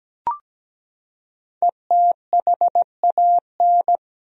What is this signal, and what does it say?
Morse code, heard as a keyed tone, about 17 wpm, 700 Hz. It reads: ETHAN
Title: USA Common Names - 17wpm